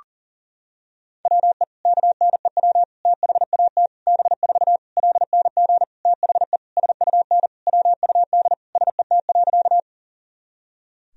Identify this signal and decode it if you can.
Morse code, heard as a keyed tone, about 40 wpm, 700 Hz. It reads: WE KNEW THAT B4 LNG THE SUN WUD SET.
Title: Sentences from Top 400 Words and Abbreviations 40wpm